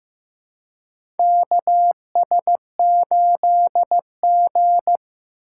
Morse code, heard as a keyed tone, about 15 wpm, 700 Hz. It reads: KS8G
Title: USA Call Signs - 4 Characters - 4x Character Spacing - 15wpm